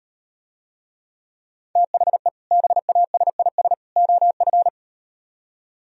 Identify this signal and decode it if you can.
Morse code, heard as a keyed tone, about 38 wpm, 700 Hz. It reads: THE BASIS OF